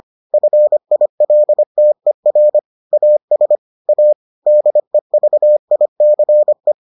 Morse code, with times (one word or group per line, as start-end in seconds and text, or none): 0.3-2.6 FILTER
2.9-3.6 AS
3.9-4.1 A
4.5-6.7 DEVICE